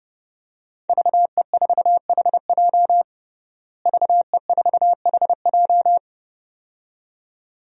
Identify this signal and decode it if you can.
Morse code, heard as a keyed tone, about 30 wpm, 700 Hz. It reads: VE4HJ VE4HJ
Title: DXCC-100 Call Signs - 5 Characters - Repeated 2x - 30wpm